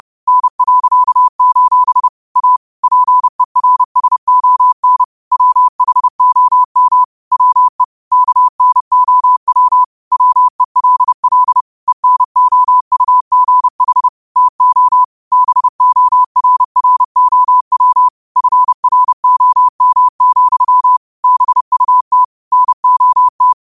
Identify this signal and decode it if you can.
Morse code, heard as a keyed tone, about 30 wpm, 1000 Hz. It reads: N.: A PERSON WHOM WE KNOW WELL ENOUGH TO BORROW FROM, BUT NOT
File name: qotdfiles / 30wpm